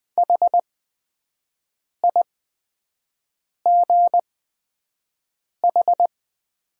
Morse code, heard as a keyed tone, about 20 wpm, spaced as for 5 wpm, 700 Hz. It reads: HIGH
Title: 8x Character Spacing - Top 200 Words - 20wpm